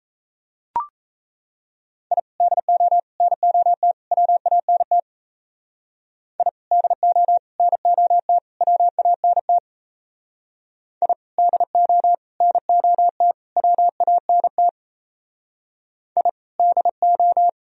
Morse code, heard as a keyed tone, about 34 wpm, 700 Hz. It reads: I DO NOT WANT I DO NOT WANT I DO NOT WANT I DO
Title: Speed-Racing - Sets of 4 Words 28wpm